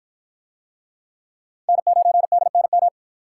Morse code, heard as a keyed tone, about 53 wpm, 700 Hz. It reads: N9DNM